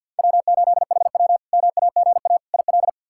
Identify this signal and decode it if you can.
Morse code, heard as a keyed tone, about 52 wpm, 700 Hz. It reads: W8HW MAGA IL
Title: K1USN SST QSO Practice - 50wpm